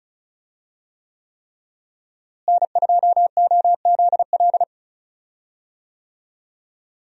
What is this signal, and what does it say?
Morse code, heard as a keyed tone, about 36 wpm, 700 Hz. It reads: N2OZL
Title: USA Call Signs - 5 Characters - Repeated 3x - 35wpm